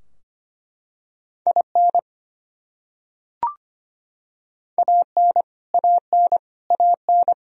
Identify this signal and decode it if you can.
Morse code, heard as a keyed tone, about 25 wpm, 700 Hz. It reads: IN AN AN AN
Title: Top 100 Words Repeated 25wpm